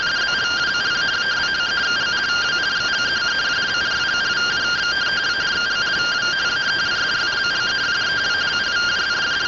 Two-tone teleprinter signal, about 170 Hz shift.